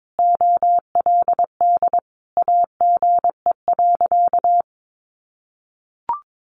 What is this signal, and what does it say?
Morse code, heard as a keyed tone, about 22 wpm, 700 Hz. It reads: OLD AGE.
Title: Sentences from Top 1300 Words 22wpm